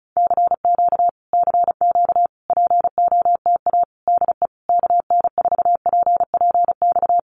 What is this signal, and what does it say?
Morse code, heard as a keyed tone, about 34 wpm, 700 Hz. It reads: CQ CQ POTA DE KN4PPX
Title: Common QSO Phrases Encore 35wpm